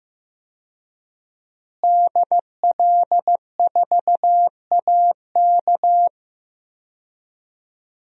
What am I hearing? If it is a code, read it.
Morse code, heard as a keyed tone, about 15 wpm, 700 Hz. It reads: DL4AK
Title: DXCC-100 Call Signs - 5 Characters - Repeated 2x - 15wpm